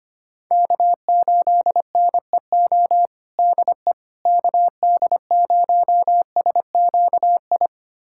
Morse code, heard as a keyed tone, about 25 wpm, 700 Hz. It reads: K8NEO DE KD0SQI
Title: USA Callsign DE Callsign 25wpm